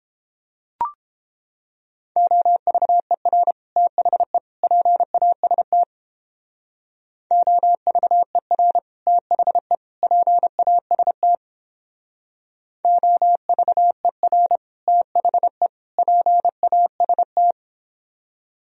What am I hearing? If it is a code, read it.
Morse code, heard as a keyed tone, about 31 wpm, 700 Hz. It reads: OVER THE PAST OVER THE PAST OVER THE PAST